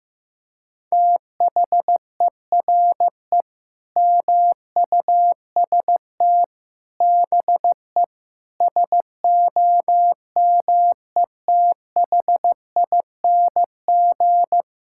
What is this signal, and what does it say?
Morse code, heard as a keyed tone, about 15 wpm, 700 Hz. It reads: THERE MUST BE SOMETHING
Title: Sentences from Top 1000 Words - Review Part 6 - 15wpm